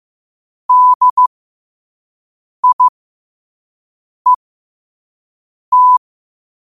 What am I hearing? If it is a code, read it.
Morse code, heard as a keyed tone, about 15 wpm, spaced as for 5 wpm, 1000 Hz. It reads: DIET
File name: qotdfiles / 5wpm